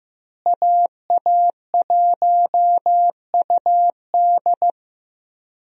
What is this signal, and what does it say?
Morse code, heard as a keyed tone, about 15 wpm, 700 Hz. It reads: AA1UD
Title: USA Call Signs - 5 Characters - Repeated 2x - 15wpm